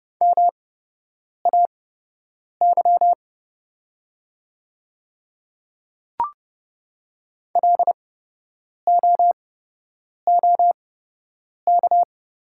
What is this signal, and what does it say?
Morse code, heard as a keyed tone, about 30 wpm, spaced as for 8 wpm, 700 Hz. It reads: MAY LOOK
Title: 8x Character Spacing - Top 100 Words - 30wpm